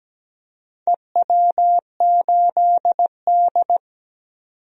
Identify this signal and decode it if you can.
Morse code, heard as a keyed tone, about 17 wpm, 700 Hz. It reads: EW8D
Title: DXCC-100 Call Signs - 4 Characters - 17wpm